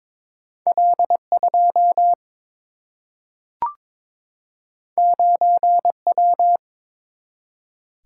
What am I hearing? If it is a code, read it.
Morse code, heard as a keyed tone, about 22 wpm, 700 Hz. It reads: L 2 9 W
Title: Single-Letter Single-Number 22wpm